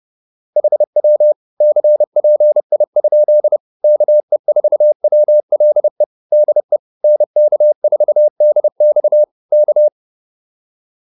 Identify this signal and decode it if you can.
Morse code, heard as a keyed tone, about 30 wpm, 600 Hz. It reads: HW CPI? KE4WLE DE NK4DX K